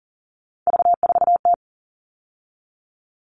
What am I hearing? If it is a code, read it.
Morse code, heard as a keyed tone, about 40 wpm, 700 Hz. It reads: V4T
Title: All Cut Numbers - 3-Digit 40wpm